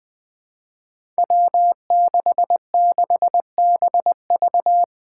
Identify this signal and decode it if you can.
Morse code, heard as a keyed tone, about 20 wpm, 700 Hz. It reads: W66BV
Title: Five Challenging Letter-Number Combinations 20wpm